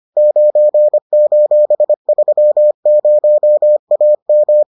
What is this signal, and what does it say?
Morse code, heard as a keyed tone, about 25 wpm, 600 Hz. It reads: 9:30AM